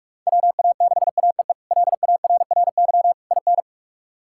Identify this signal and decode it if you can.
Morse code, heard as a keyed tone, about 46 wpm, 700 Hz. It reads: WA6AEE LARRY IN
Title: K1USN SST QSO Practice - 45wpm